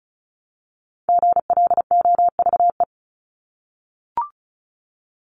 Morse code, heard as a keyed tone, about 34 wpm, 700 Hz. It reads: GLOVE
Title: Top 1751-2000 Words 35wpm